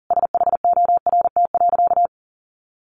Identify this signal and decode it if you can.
Morse code, heard as a keyed tone, about 40 wpm, 700 Hz. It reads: SHORT.
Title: Sentences from Top 500 Words 40wpm